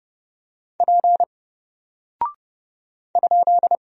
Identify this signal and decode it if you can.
Morse code, heard as a keyed tone, about 31 wpm, 700 Hz. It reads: P ?
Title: Single Character (Letter:Number:Punctuation:Prosign) 30wpm